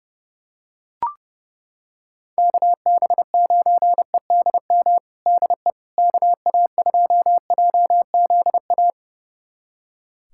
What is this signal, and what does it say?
Morse code, heard as a keyed tone, about 32 wpm, 700 Hz. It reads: KB9EDM DE KA2JZA
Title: USA Callsign DE Callsign 30wpm